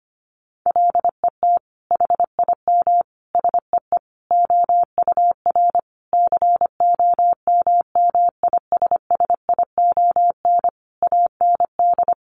Morse code, heard as a keyed tone, about 25 wpm, 700 Hz. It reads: LET HIM SEE OUR COMMISSION AND
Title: Sentences from Top 1400 Words 25wpm